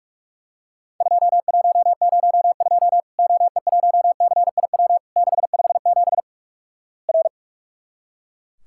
Morse code, heard as a keyed tone, about 44 wpm, 700 Hz. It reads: 2102 OE1CIW 657 R
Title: Worked All Europe DX Contest - QTC Exchange - 45wpm